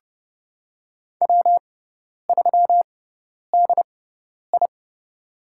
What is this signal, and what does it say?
Morse code, heard as a keyed tone, about 31 wpm, spaced as for 10 wpm, 700 Hz. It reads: W3DI
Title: USA Call Signs - 4 Characters - 6x Character Spacing - 30wpm